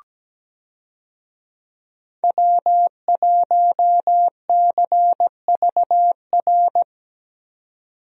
Morse code, heard as a keyed tone, about 17 wpm, 700 Hz. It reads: W1CVR
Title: USA Call Signs - 5 Characters - 17wpm